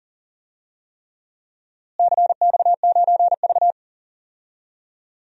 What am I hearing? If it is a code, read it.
Morse code, heard as a keyed tone, about 40 wpm, 700 Hz. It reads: CX9V
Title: DXCC-100 Call Signs - 4 Characters - Repeated 3x - 40wpm